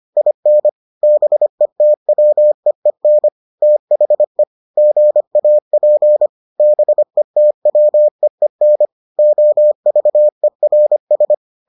Morse code, heard as a keyed tone, about 25 wpm, 600 Hz. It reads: IN BETWEEN THE GAP BETWEEN OVERS